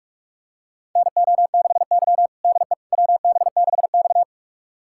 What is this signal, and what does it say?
Morse code, heard as a keyed tone, about 44 wpm, 690 Hz. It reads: NO6Y DE WB6X
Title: USA Callsign DE Callsign - 4 Characters - 45wpm